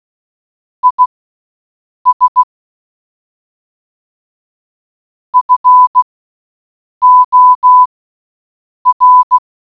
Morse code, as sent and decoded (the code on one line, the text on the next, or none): .. ... / ..-. --- .-.
IS FOR